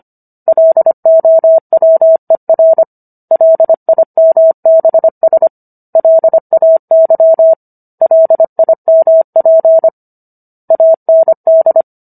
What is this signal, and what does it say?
Morse code, heard as a keyed tone, about 25 wpm, 650 Hz. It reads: LOWER LIMBS LAY LIMP AND